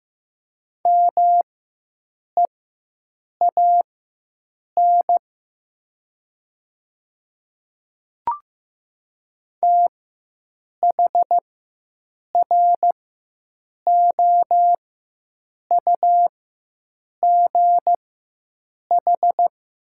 Morse code, heard as a keyed tone, about 15 wpm, spaced as for 7 wpm, 700 Hz. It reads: MEAN THROUGH
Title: 4x Character Spacing - Top 300 Words - 15wpm